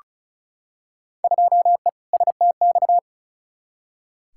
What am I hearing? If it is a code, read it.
Morse code, heard as a keyed tone, about 36 wpm, 700 Hz. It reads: 2E STX
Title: Field Day Exchanges Encore - 35wpm